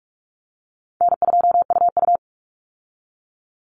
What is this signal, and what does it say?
Morse code, heard as a keyed tone, about 44 wpm, 700 Hz. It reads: N2UU
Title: Easy Call Signs - 45wpm